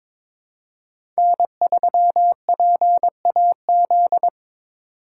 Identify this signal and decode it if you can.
Morse code, heard as a keyed tone, about 22 wpm, 700 Hz. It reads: N3PAZ
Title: USA Call Signs - 5 Characters - 2x Character Spacing - 22wpm